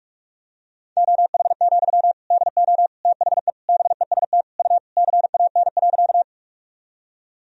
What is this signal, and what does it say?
Morse code, heard as a keyed tone, about 41 wpm, 700 Hz. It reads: OH, DO THE BEST U CAN.
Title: Sentences from Top 500 Words and Abbreviations 45wpm